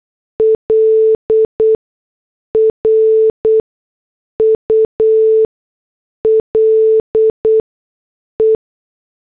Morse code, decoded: L RULE